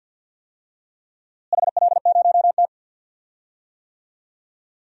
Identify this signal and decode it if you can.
Morse code, heard as a keyed tone, about 52 wpm, 700 Hz. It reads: HL0T
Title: DXCC-100 Call Signs - 4 Characters - Repeated 3x - 50wpm